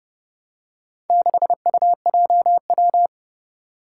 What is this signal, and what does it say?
Morse code, heard as a keyed tone, about 30 wpm, 700 Hz. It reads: 6UJW
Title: Four Challenging Letter-Number Combinations 30wpm